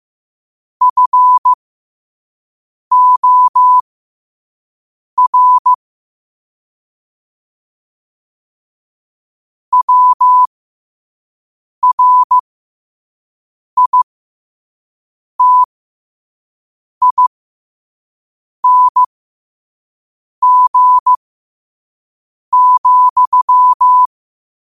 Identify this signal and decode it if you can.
Morse code, heard as a keyed tone, about 15 wpm, spaced as for 5 wpm, 1000 Hz. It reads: FOR WRITING,
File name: qotdfiles / 5wpm